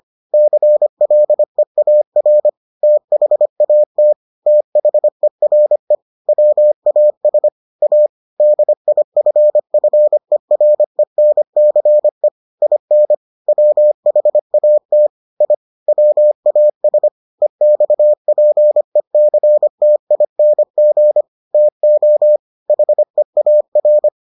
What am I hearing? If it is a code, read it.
Morse code, heard as a keyed tone, about 25 wpm, 600 Hz. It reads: CLEAR THAT THERE WAS A DIFFERENCE IN WHAT I WAS EXPECTING TO HEAR